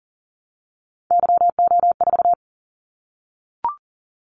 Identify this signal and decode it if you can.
Morse code, heard as a keyed tone, about 42 wpm, 700 Hz. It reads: YO4
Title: Callsign Prefixes for Top 25 DXCC - 40wpm